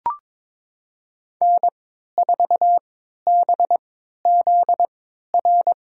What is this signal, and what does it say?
Morse code, heard as a keyed tone, about 22 wpm, spaced as for 13 wpm, 700 Hz. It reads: N4BZR
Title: USA Call Signs - 5 Characters - 3x Character Spacing - 22wpm